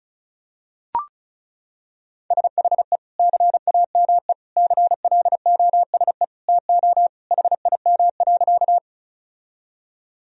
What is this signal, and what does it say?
Morse code, heard as a keyed tone, about 34 wpm, 700 Hz. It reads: SHE CAME CLOSE TO HIM.
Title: Sentences from Top 1000 Words - Review Part 1 - 35wpm